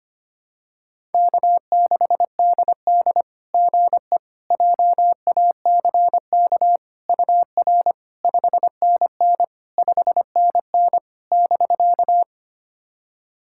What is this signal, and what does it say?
Morse code, heard as a keyed tone, about 25 wpm, 700 Hz. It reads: K6DD GE JACK UR 5NN 5NN <BK>
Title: SOTA QSO Practice 25wpm